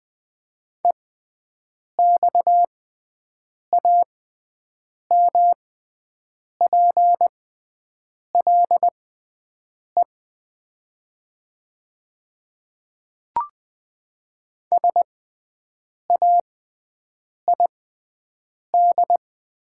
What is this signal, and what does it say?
Morse code, heard as a keyed tone, about 20 wpm, spaced as for 7 wpm, 700 Hz. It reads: EXAMPLE SAID
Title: 6x Character Spacing - Top 300 Words - 20wpm